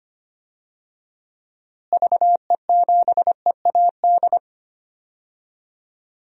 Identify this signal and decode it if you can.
Morse code, heard as a keyed tone, about 25 wpm, 700 Hz. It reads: VE7EAD
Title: DXCC-100 Call Signs - 6 Characters - Repeated 3x - 25wpm